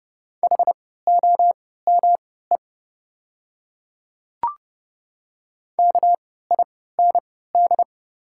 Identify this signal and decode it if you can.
Morse code, heard as a keyed tone, about 30 wpm, spaced as for 17 wpm, 700 Hz. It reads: HOME KIND
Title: 3x Character Spacing - Top 200 Words - 30wpm